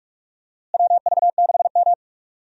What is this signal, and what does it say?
Morse code, heard as a keyed tone, about 44 wpm, 700 Hz. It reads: WV6M
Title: USA Call Signs - 4 Characters - Repeated 3x - 45wpm